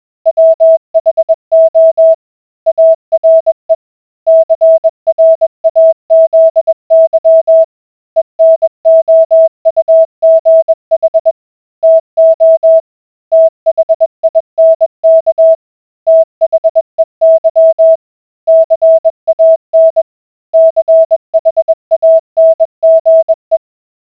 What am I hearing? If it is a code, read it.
Morse code, heard as a keyed tone, about 21 wpm, 650 Hz. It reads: WHO ARE CRAZY ENOUGH TO THINK THEY CAN CHANGE